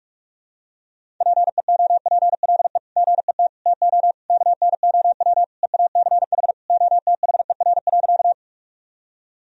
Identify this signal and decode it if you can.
Morse code, heard as a keyed tone, about 42 wpm, 700 Hz. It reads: PEOPLE GET TO KNOW EACH OTHER.
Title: Sentences from Top 1000 Words - Review Part 1 - 45wpm